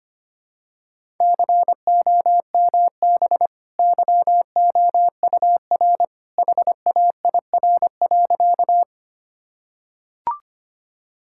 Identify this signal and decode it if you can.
Morse code, heard as a keyed tone, about 25 wpm, 700 Hz. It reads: COMB YOUR HAIR.
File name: Easy Sentences with Hints - 25wpm